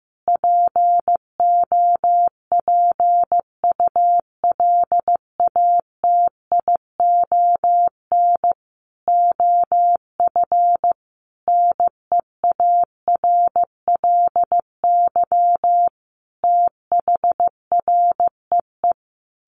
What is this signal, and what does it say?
Morse code, heard as a keyed tone, about 15 wpm, 700 Hz. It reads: POPULATION OF NEARLY THREE
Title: Sentences from Top 1300 Words 15wpm